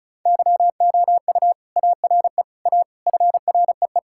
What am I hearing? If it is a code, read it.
Morse code, heard as a keyed tone, about 36 wpm, 700 Hz. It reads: YOU ARE A FREE
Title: Sentences from Top 1600 Words 35wpm